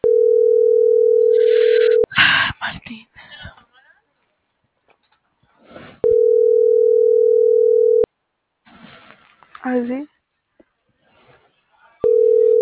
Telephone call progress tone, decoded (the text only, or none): ringback tone